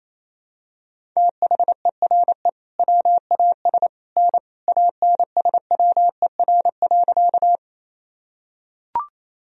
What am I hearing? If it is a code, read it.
Morse code, heard as a keyed tone, about 27 wpm, 700 Hz. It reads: THERE WAS N ANSWER.
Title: Sentences from Top 300 Words and Abbreviations 28wpm